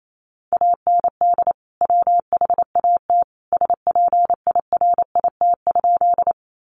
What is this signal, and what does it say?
Morse code, heard as a keyed tone, about 28 wpm, 700 Hz. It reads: AND WHAT SPIRIT?
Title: Sentences from Top 1400 Words 28wpm